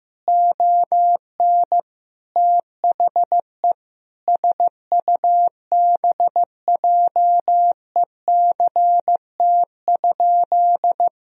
Morse code, heard as a keyed tone, about 15 wpm, 700 Hz. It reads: ON THE SUBJECT?
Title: Sentences from Top 1000 Words - Review Part 4 - 15wpm